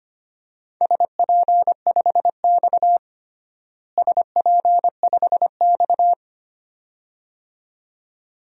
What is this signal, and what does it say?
Morse code, heard as a keyed tone, about 25 wpm, 700 Hz. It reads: SP5X SP5X
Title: DXCC-100 Call Signs - 4 Characters - Repeated 2x - 25wpm